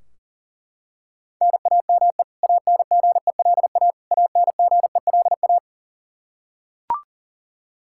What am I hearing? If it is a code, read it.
Morse code, heard as a keyed tone, about 40 wpm, 700 Hz. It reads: NAME ANGELA ANGELA
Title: Practice QSO - Exchange 1 - Beginning - 40wpm